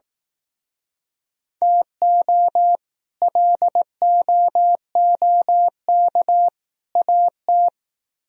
Morse code, heard as a keyed tone, about 18 wpm, 700 Hz. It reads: TO LOOK AT
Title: Speed-Racing - Sets of 3 Words 15wpm